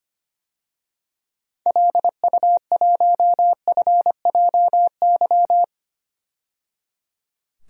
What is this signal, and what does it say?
Morse code, heard as a keyed tone, about 25 wpm, 700 Hz. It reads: LU1FJY
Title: DXCC-100 Call Signs - 6 Characters - 25wpm